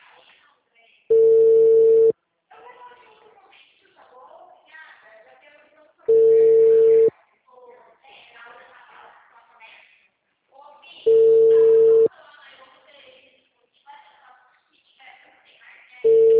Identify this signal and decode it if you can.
Ringback tone (China, 450 Hz), with speech.